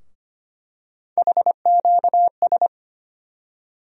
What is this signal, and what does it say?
Morse code, heard as a keyed tone, about 25 wpm, 700 Hz. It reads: HQS
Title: USA Call Sign Postfixes - 3 Letters 25wpm